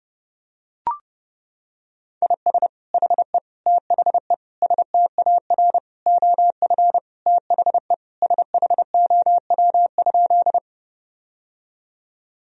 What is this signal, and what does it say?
Morse code, heard as a keyed tone, about 30 wpm, 700 Hz. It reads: IS HE THE STAR OF THE SHOW?
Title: Sentences from Top 1000 Words - Review Part 2 - 30wpm